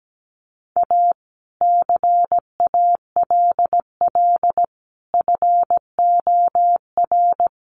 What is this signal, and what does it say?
Morse code, heard as a keyed tone, about 17 wpm, 700 Hz. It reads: A CALL FOR
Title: Sentences from Top 300 Words 17wpm